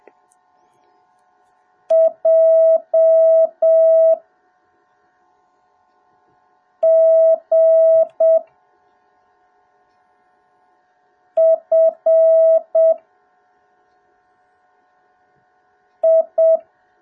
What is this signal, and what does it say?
Morse code, heard as a keyed tone, about 7 wpm, 650 Hz. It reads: J G F I